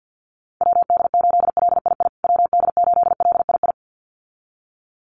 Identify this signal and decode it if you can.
Morse code, heard as a keyed tone, about 50 wpm, 700 Hz. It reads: WD8LII WD8LII
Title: USA Call Signs Repeated 2x 50wpm